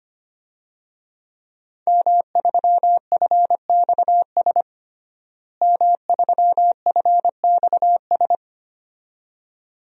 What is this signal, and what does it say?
Morse code, heard as a keyed tone, about 25 wpm, 700 Hz. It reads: M3FXS M3FXS
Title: DXCC-100 Call Signs - 5 Characters - Repeated 2x - 25wpm